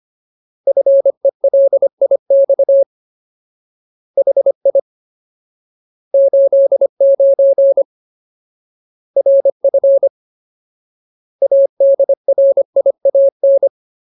Morse code, heard as a keyed tone, about 25 wpm, 550 Hz. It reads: FELIX HI 89 RF ADRIAN
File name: Ultimate Word List / 25WPM Ultimate